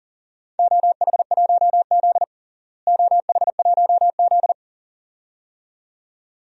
Morse code, heard as a keyed tone, about 37 wpm, 700 Hz. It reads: OH1Z OH1Z
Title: DXCC-100 Call Signs - 4 Characters - Repeated 2x - 40wpm